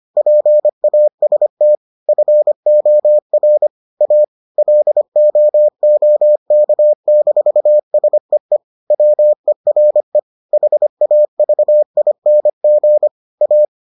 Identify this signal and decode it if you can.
Morse code, heard as a keyed tone, about 25 wpm, 600 Hz. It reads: PAST FOR A LOOK-SEE WERE HAVING A